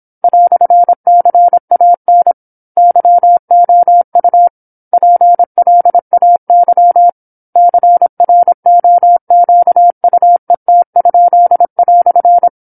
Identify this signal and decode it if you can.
Morse code, heard as a keyed tone, about 26 wpm, 700 Hz. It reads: "CAN YOU PLAY CROQUET?"
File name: 08-Chapter-08 26 WPM